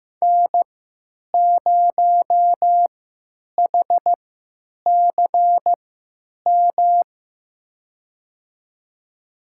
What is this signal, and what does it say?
Morse code, heard as a keyed tone, about 15 wpm, spaced as for 9 wpm, 700 Hz. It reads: N0HCM
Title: USA Call Signs - 5 Characters - 3x Character Spacing - 15wpm